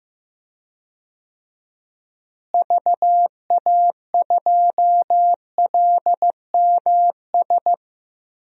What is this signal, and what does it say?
Morse code, heard as a keyed tone, about 15 wpm, 700 Hz. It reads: VA2LMS
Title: DXCC-100 Call Signs - 6 Characters - Repeated 2x - 15wpm